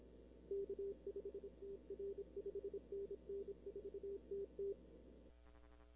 Morse code, heard as a keyed tone, about 29 wpm, 440 Hz. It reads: K5TR 5NN 4TT
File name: arrl cw dx 2005